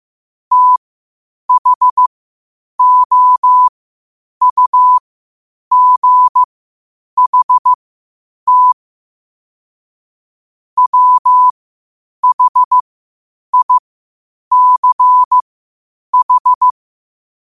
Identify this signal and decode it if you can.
Morse code, heard as a keyed tone, about 15 wpm, spaced as for 8 wpm, 1000 Hz. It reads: THOUGHT WHICH
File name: qotdfiles / 8wpm